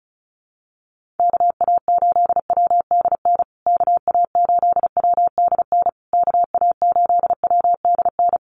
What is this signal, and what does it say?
Morse code, heard as a keyed tone, about 34 wpm, 700 Hz. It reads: KA8WDN KA8WDN KA8WDN
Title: USA Call Signs Repeated 3x 35wpm